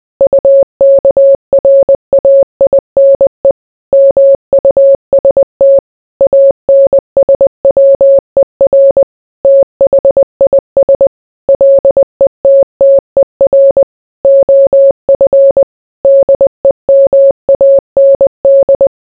Morse code, heard as a keyed tone, about 20 wpm, 550 Hz. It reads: UKRAINE MUST ANSWER THIS LETTER OF DEMAND